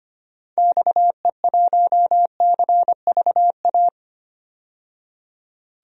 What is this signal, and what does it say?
Morse code, heard as a keyed tone, about 25 wpm, 700 Hz. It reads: XE1CVA
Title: DXCC-100 Call Signs - 6 Characters - 25wpm